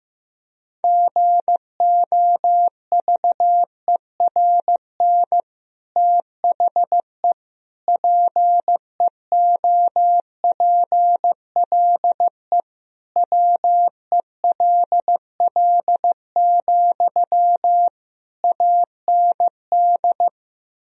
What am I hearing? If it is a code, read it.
Morse code, heard as a keyed tone, about 15 wpm, 700 Hz. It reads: GOVERN THE PEOPLE WELL, AND
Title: Sentences from Top 1000 Words - Review Part 2 - 15wpm